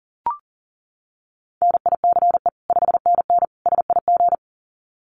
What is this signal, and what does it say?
Morse code, heard as a keyed tone, about 40 wpm, 700 Hz. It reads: NICE 5NN SIG
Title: Common QSO Phrases 40wpm